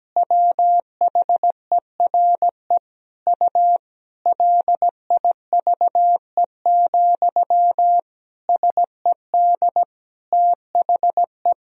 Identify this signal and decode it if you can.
Morse code, heard as a keyed tone, about 17 wpm, 700 Hz. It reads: WHERE U LIVE, SED THE
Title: Sentences from Top 200 Words and Abbreviations 17wpm